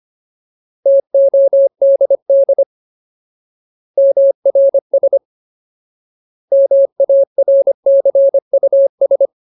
Morse code, heard as a keyed tone, about 25 wpm, 550 Hz. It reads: TODD MRS MARCUS
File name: Ultimate Word List / 25WPM Ultimate